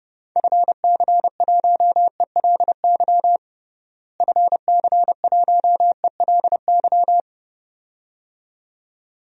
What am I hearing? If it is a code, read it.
Morse code, heard as a keyed tone, about 32 wpm, 700 Hz. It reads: FC1ELY FC1ELY
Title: DXCC-100 Call Signs - 6 Characters - Repeated 2x - 30wpm